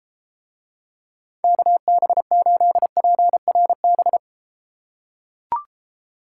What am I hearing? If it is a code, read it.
Morse code, heard as a keyed tone, about 33 wpm, 700 Hz. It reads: KB8PRB